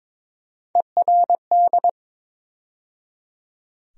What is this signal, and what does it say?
Morse code, heard as a keyed tone, about 22 wpm, 700 Hz. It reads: ERD
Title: USA Call Sign Postfixes - 3 Letters 22wpm